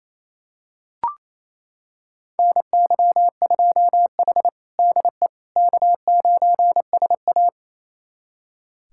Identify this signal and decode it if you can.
Morse code, heard as a keyed tone, about 28 wpm, 700 Hz. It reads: NY2H DE K9SA
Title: USA Callsign DE Callsign - 4 Characters - 28wpm